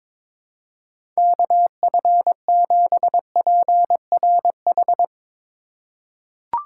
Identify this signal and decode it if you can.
Morse code, heard as a keyed tone, about 22 wpm, 700 Hz. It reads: KF7PRH
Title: USA Call Signs - 6 Characters - 2x Character Spacing - 22wpm